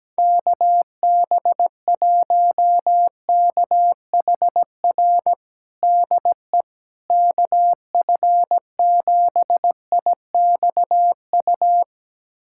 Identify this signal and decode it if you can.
Morse code, heard as a keyed tone, about 17 wpm, 700 Hz. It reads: KB1KHR DE KF7IXU
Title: USA Callsign DE Callsign - 6 Characters - 17wpm